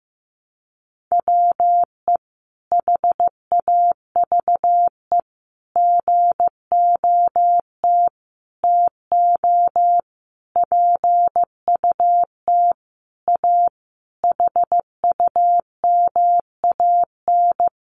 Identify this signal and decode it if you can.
Morse code, heard as a keyed tone, about 15 wpm, 700 Hz. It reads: WE HAVE GOT TO PUT A HUMAN
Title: Sentences from Top 1200 Words 15wpm